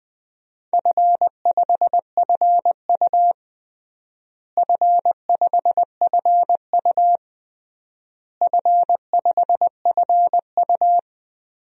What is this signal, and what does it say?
Morse code, heard as a keyed tone, about 20 wpm, 700 Hz. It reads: F5FU F5FU F5FU
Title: DXCC-100 Call Signs - 4 Characters - Repeated 3x - 20wpm